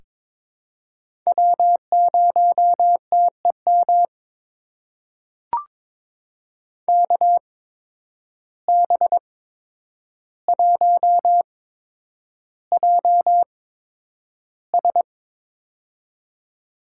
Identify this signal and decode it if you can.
Morse code, heard as a keyed tone, about 22 wpm, spaced as for 9 wpm, 700 Hz. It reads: W0TEM KB1JS
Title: USA Call Signs - 5 Characters - 8x Character Spacing - 22wpm